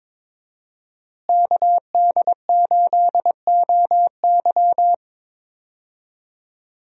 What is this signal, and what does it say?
Morse code, heard as a keyed tone, about 22 wpm, 700 Hz. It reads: KD8OY
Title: USA Call Signs - 5 Characters - Repeated 3x - 22wpm